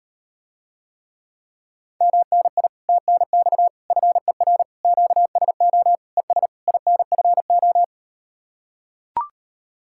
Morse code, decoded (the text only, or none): MNI TNX FER QSO ES INFO